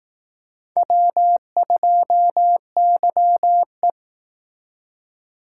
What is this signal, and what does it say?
Morse code, heard as a keyed tone, about 18 wpm, 700 Hz. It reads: W2YE
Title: Speed Racing - USA 1x2 Call Signs 15wpm